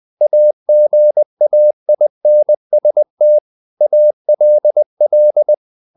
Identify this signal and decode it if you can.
Morse code, heard as a keyed tone, about 20 wpm, 600 Hz. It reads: AGAINST ALL
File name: the old man and the sea / the old man and the sea part 1 20wpm